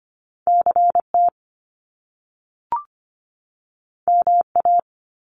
Morse code, heard as a keyed tone, about 25 wpm, 700 Hz. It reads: CT MA
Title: US States 25wpm